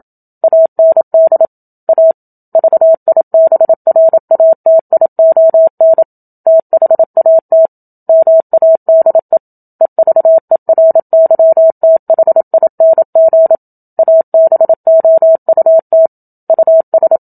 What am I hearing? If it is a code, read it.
Morse code, heard as a keyed tone, about 27 wpm, 650 Hz. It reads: AND A VIBRATION THAT MADE EVERYTHING ABOUT US
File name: Book Two / 01-Chapter 1 - 27 WPM